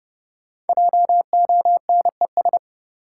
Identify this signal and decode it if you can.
Morse code, heard as a keyed tone, about 31 wpm, 700 Hz. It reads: JONES
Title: Sets of 5 Words - New Word Drill - 30wpm